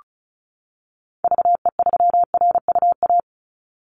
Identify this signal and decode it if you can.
Morse code, heard as a keyed tone, about 36 wpm, 700 Hz. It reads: VE3RUA
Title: Call Signs - 35wpm